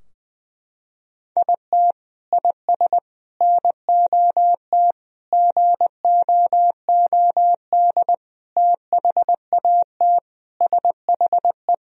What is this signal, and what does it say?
Morse code, heard as a keyed tone, about 20 wpm, 700 Hz. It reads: IT IS NOT GOOD THAT SHE
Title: Sentences from Top 1000 Words - Review Part 4 - 20wpm